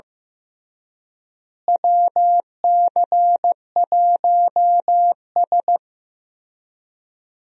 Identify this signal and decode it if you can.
Morse code, heard as a keyed tone, about 16 wpm, 700 Hz. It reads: WC1S
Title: USA Call Signs - 4 Characters - Repeated 3x - 15wpm